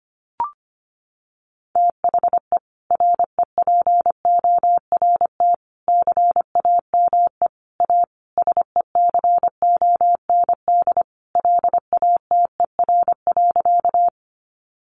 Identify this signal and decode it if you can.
Morse code, heard as a keyed tone, about 25 wpm, 700 Hz. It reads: THE REPORT CAME A SECOND LATER.
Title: Sentences from Top 900 Words 25wpm